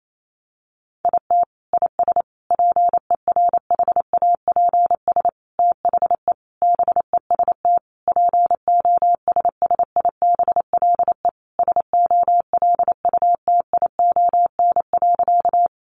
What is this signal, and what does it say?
Morse code, heard as a keyed tone, about 28 wpm, 700 Hz. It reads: IT IS PERHAPS THE BEST POSSIBLE SOLUTION.
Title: Sentences from Top 1300 Words 28wpm